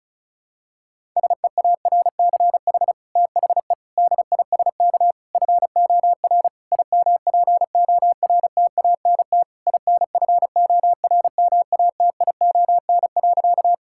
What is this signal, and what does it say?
Morse code, heard as a keyed tone, about 36 wpm, 700 Hz. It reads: SEARCH THE DISK FOR IMPORTANT INFORMATION.
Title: Sentences from Top 1000 Words - Review Part 6 - 35wpm